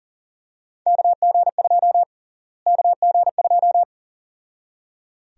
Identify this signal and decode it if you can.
Morse code, heard as a keyed tone, about 39 wpm, 700 Hz. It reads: KG2 KG2
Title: USA Call Sign Prefixes - Repeated 2x - 40wpm